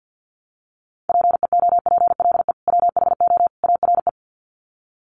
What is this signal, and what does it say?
Morse code, heard as a keyed tone, about 50 wpm, 700 Hz. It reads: PEOPLE WHO ARE
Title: Sets of 3 Words 50wpm